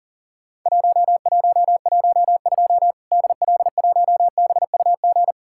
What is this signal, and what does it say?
Morse code, heard as a keyed tone, about 40 wpm, 700 Hz. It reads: 1112 DL1BUG
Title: Worked All Europe DX Contest - QTC Exchange - 40wpm